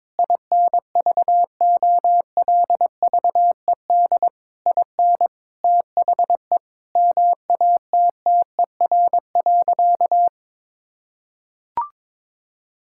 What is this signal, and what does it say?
Morse code, heard as a keyed tone, about 21 wpm, 700 Hz. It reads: INVOLVED IN THE MATTER.
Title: Sentences from Top 1000 Words - Review Part 5 - 22wpm